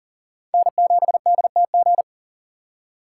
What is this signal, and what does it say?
Morse code, heard as a keyed tone, about 52 wpm, 700 Hz. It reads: N7DTG